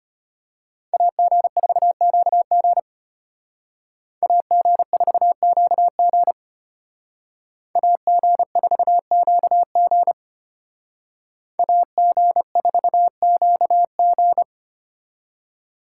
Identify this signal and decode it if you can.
Morse code, heard as a keyed tone, about 32 wpm, 700 Hz. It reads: AG4QG AG4QG AG4QG AG4QG